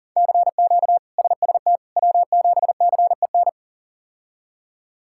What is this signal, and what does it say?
Morse code, heard as a keyed tone, about 40 wpm, 700 Hz. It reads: CQ SST W7CEN
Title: K1USN SST QSO Practice - 40wpm